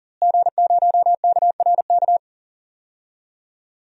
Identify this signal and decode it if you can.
Morse code, heard as a keyed tone, about 40 wpm, 700 Hz. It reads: G0KRK
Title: DXCC-100 Call Signs - 5 Characters - Repeated 3x - 40wpm